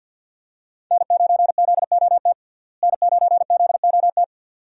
Morse code, heard as a keyed tone, about 52 wpm, 700 Hz. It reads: N9ZOT N9ZOT
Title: USA Call Signs - 5 Characters - Repeated 2x - 50wpm